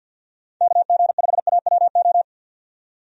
Morse code, heard as a keyed tone, about 50 wpm, 700 Hz. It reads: KG5AWO
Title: USA Call Signs - 6 Characters - 50wpm